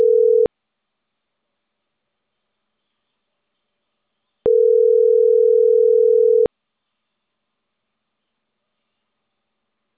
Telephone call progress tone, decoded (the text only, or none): ringback tone